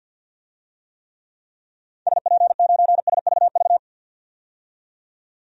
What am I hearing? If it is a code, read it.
Morse code, heard as a keyed tone, about 45 wpm, 700 Hz. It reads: SP9SVV
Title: DXCC-100 Call Signs - 6 Characters - 50wpm